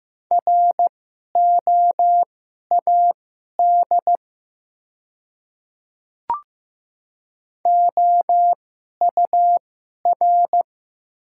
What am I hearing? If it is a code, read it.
Morse code, heard as a keyed tone, about 15 wpm, spaced as for 11 wpm, 700 Hz. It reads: ROAD OUR
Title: 2x Character Spacing - Top 500 Words - 15wpm